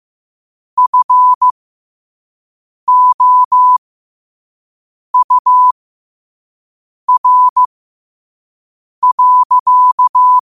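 Morse code, heard as a keyed tone, about 15 wpm, spaced as for 5 wpm, 1000 Hz. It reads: FOUR.
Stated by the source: Morse code quote of the day at 5 wpm.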